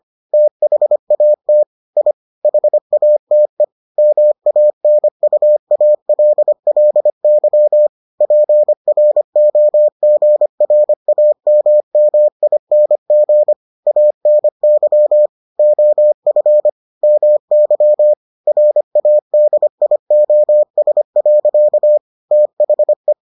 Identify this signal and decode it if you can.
Morse code, heard as a keyed tone, about 25 wpm, 600 Hz. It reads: THAT I HATE MANUALLY PROGRAMMING ANY OF MY RADIOS. THE